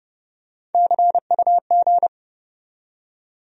Morse code, heard as a keyed tone, about 30 wpm, 700 Hz. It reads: CUG
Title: USA Call Sign Postfixes - 3 Letters 30wpm